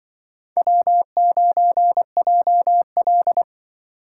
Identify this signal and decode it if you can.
Morse code, heard as a keyed tone, about 24 wpm, 700 Hz. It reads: W9JL
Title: Speed Racing - USA 1x2 Call Signs 20wpm